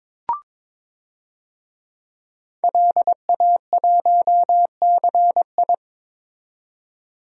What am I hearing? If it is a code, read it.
Morse code, heard as a keyed tone, about 22 wpm, 700 Hz. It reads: LA1CI
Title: DXCC-100 Call Signs - 5 Characters - 22wpm